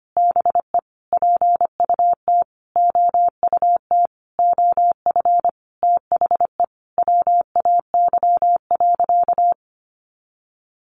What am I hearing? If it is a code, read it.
Morse code, heard as a keyed tone, about 25 wpm, 700 Hz. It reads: BE PUT OUT OF THE WAY.
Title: Sentences from Top 900 Words 25wpm